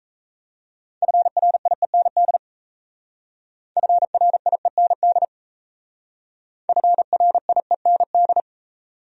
Morse code, heard as a keyed tone, about 35 wpm, 700 Hz. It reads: FRIEND FRIEND FRIEND
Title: Speed-Racing Top 500 Words 28wpm